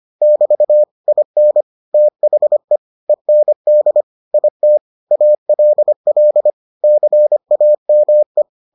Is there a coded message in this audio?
Yes: Morse code from 0.2 to 8.4 s, heard as a keyed tone, about 25 wpm, 600 Hz. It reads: <BT> IN THE END IT ALL CAME